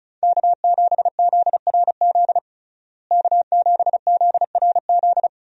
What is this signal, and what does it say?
Morse code, heard as a keyed tone, about 34 wpm, 700 Hz. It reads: K7ZRZ K7ZRZ
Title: USA Call Signs - 5 Characters - Repeated 2x - 35wpm